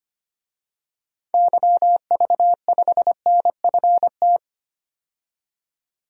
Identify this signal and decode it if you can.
Morse code, heard as a keyed tone, about 25 wpm, 700 Hz. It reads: YV5NFT
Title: DXCC-100 Call Signs - 6 Characters - 25wpm